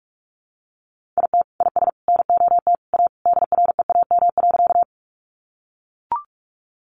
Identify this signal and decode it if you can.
Morse code, heard as a keyed tone, about 44 wpm, 700 Hz. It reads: IT IS NOT A DREAM.
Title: Sentences from Top 1500 Words 45wpm